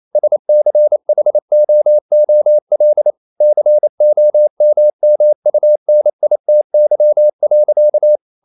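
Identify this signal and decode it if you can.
Morse code, heard as a keyed tone, about 28 wpm, 600 Hz. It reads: SCHOOL COMMUNITY.
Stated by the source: Morse code practice at 28 wpm, text of Ken Follett's novel Never in English, from CW med Gnister.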